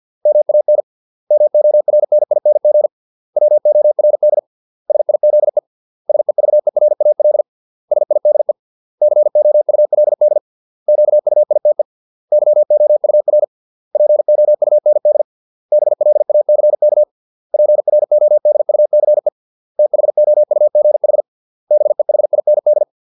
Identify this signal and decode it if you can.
Morse code, heard as a keyed tone, about 50 wpm, 600 Hz. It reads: MAN MORNING WORD SIZE SEVERAL SIDE COULD QUITE YOUR POUND BLACK PRODUCE THOUGH BEHIND